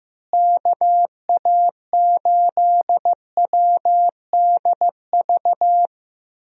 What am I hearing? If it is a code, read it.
Morse code, heard as a keyed tone, about 15 wpm, 700 Hz. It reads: KA8WDV
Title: USA Call Signs - 6 Characters - Repeated 3x - 15wpm